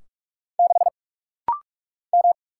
Morse code, heard as a keyed tone, about 44 wpm, 700 Hz. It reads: 6 M
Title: Single Letter-Number - Rapid-Fire 45wpm